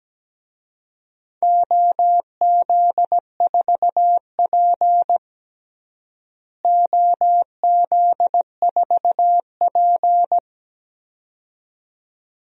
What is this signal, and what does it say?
Morse code, heard as a keyed tone, about 17 wpm, 700 Hz. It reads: OZ4P OZ4P
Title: DXCC-100 Call Signs - 4 Characters - Repeated 2x - 17wpm